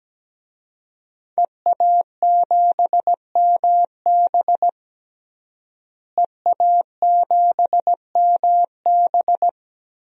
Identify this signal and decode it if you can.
Morse code, heard as a keyed tone, about 17 wpm, 700 Hz. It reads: EA7MB EA7MB
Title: DXCC-100 Call Signs - 5 Characters - Repeated 2x - 17wpm